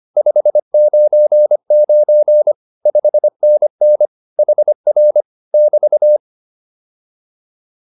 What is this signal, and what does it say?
Morse code, heard as a keyed tone, about 25 wpm, 600 Hz. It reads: 599 5NN HR <BT>
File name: QSO205 25